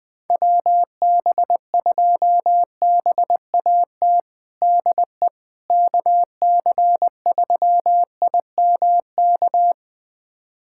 Morse code, heard as a keyed tone, about 20 wpm, 700 Hz. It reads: WB2BAT DE KC3IMK
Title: USA Callsign DE Callsign - 6 Characters - 20wpm